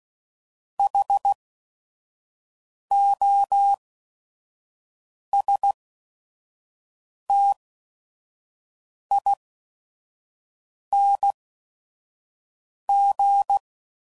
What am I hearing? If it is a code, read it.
Morse code, heard as a keyed tone, about 16 wpm, spaced as for 5 wpm, 800 Hz. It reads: HOSTING
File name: Suffixes / Special ING's / VerySlowToFaster ING's